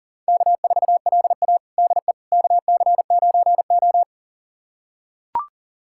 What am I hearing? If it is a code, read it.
Morse code, heard as a keyed tone, about 40 wpm, 700 Hz. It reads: K4LA DE KC9O
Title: USA Callsign DE Callsign - 4 Characters - 40wpm